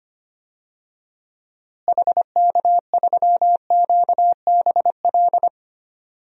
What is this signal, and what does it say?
Morse code, heard as a keyed tone, about 25 wpm, 700 Hz. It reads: HK3QBL
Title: DXCC-100 Call Signs - 6 Characters - Repeated 2x - 25wpm